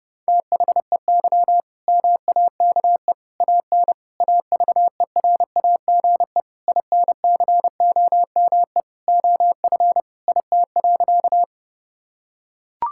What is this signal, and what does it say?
Morse code, heard as a keyed tone, about 32 wpm, 700 Hz. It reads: THEY MAKE AN AVERAGE INCOME OF IT.
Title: Sentences from Top 1000 Words - Review Part 6 - 30wpm